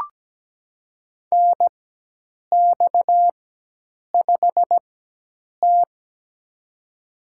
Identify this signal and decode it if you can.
Morse code, heard as a keyed tone, about 17 wpm, spaced as for 8 wpm, 700 Hz. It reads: NX5T
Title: USA Call Signs - 4 Characters - 4x Character Spacing - 17wpm